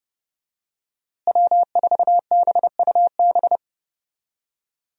Morse code, heard as a keyed tone, about 31 wpm, 700 Hz. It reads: W4BUB
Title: USA Call Signs - 5 Characters - Repeated 3x - 30wpm